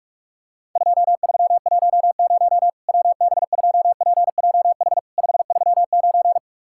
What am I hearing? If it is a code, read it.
Morse code, heard as a keyed tone, about 46 wpm, 700 Hz. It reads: 2310 WB2PJH 539
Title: Worked All Europe DX Contest - QTC Exchange - 45wpm